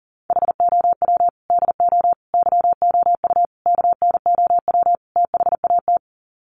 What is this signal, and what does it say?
Morse code, heard as a keyed tone, about 37 wpm, 700 Hz. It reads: HOW DO YOU KNOW THAT
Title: Sets of 5 Words Encore 40wpm